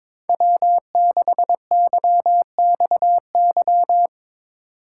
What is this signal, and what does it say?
Morse code, heard as a keyed tone, about 22 wpm, 700 Hz. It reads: W6YXY
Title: USA Call Signs - 5 Characters - 2x Character Spacing - 22wpm